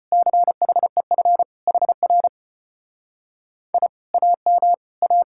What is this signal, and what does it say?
Morse code, heard as a keyed tone, about 33 wpm, 700 Hz. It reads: CHEF HR I AM A